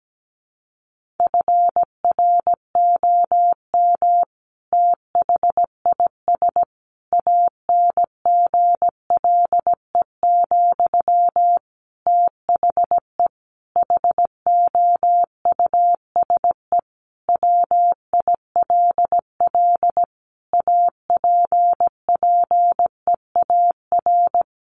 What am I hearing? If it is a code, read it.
Morse code, heard as a keyed tone, about 17 wpm, 700 Hz. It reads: FROM THIS ANGLE, THE HOUSE WILL APPEAR
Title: Sentences from Top 1500 Words 17wpm